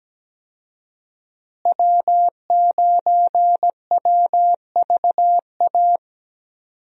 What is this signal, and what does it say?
Morse code, heard as a keyed tone, about 17 wpm, 700 Hz. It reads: W9WVA
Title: USA Call Signs - 5 Characters - Repeated 2x - 17wpm